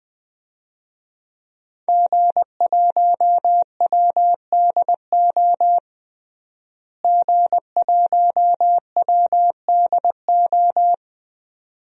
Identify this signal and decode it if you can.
Morse code, heard as a keyed tone, about 20 wpm, 700 Hz. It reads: G1WDO G1WDO
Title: DXCC-100 Call Signs - 5 Characters - Repeated 2x - 20wpm